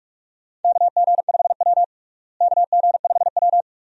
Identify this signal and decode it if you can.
Morse code, heard as a keyed tone, about 46 wpm, 700 Hz. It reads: KG5W KG5W
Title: USA Call Signs - 4 Characters - Repeated 2x - 45wpm